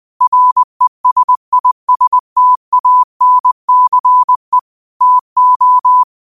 Morse code, heard as a keyed tone, about 20 wpm, 1000 Hz. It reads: RESISTANCE TO
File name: qotdfiles / 20wpm